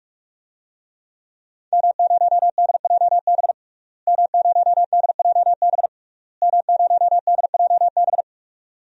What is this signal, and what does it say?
Morse code, heard as a keyed tone, about 46 wpm, 700 Hz. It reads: M0DJB M0DJB M0DJB
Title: DXCC-100 Call Signs - 5 Characters - Repeated 3x - 45wpm